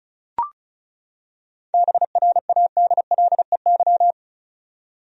Morse code, heard as a keyed tone, about 34 wpm, 700 Hz. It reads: BRADLEY
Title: USA Common Names - 35wpm